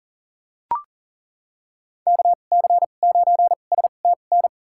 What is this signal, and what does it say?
Morse code, heard as a keyed tone, about 40 wpm, spaced as for 29 wpm, 700 Hz. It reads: KC9STN
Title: USA Call Signs - 6 Characters - 2x Character Spacing - 40wpm